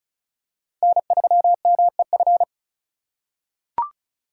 Morse code, heard as a keyed tone, about 34 wpm, 700 Hz. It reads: N3MEF
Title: USA Call Signs - 4x Character Spacing - 35wpm